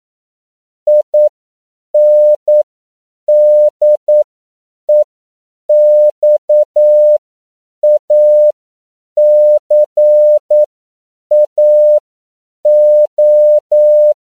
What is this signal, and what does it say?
Morse code, heard as a keyed tone, about 9 wpm, spaced as for 7 wpm, 600 Hz. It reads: INDEXACAO
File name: CW / 10PPM